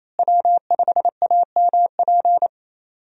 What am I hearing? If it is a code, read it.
Morse code, heard as a keyed tone, about 27 wpm, 700 Hz. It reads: W5AMP
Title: USA Call Signs - 5 Characters - Repeated 2x - 28wpm